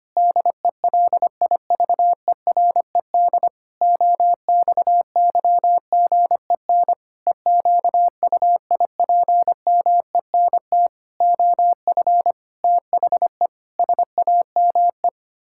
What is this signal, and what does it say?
Morse code, heard as a keyed tone, about 25 wpm, 700 Hz. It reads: DELIVERED OXYGEN EQUIPMENT OF THE SAME
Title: Pangram Challenge v2 - 25wpm